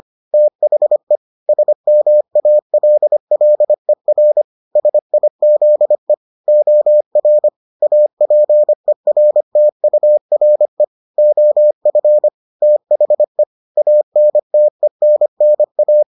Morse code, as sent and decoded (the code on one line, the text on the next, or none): - .... / ... -- .- .-.. .-.. .-. / ... .. --.. / --- .-. / .- .--. .-. - ..- .-. / --- ..-. / - .... / .- -. - . -. -. .-
THE SMALLER SIZE OR APERTURE OF THE ANTENNA